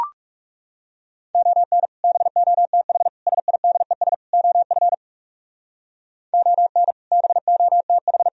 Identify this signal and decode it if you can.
Morse code, heard as a keyed tone, about 37 wpm, 700 Hz. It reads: ON BOTH SIDES OF ON BOTH
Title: Speed-Racing - Sets of 4 Words 30wpm